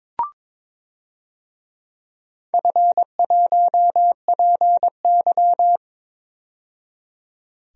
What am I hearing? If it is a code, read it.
Morse code, heard as a keyed tone, about 22 wpm, 700 Hz. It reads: F1PY
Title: DXCC-100 Call Signs - 4 Characters - 22wpm